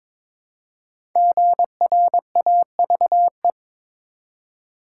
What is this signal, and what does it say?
Morse code, heard as a keyed tone, about 22 wpm, 700 Hz. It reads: GRAVE
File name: 5 Letter Words - First 3 Letters Given - 22wpm